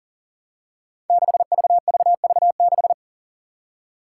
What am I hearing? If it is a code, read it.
Morse code, heard as a keyed tone, about 40 wpm, 700 Hz. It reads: 6VVV6
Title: Five Challenging Letter-Number Combinations 40wpm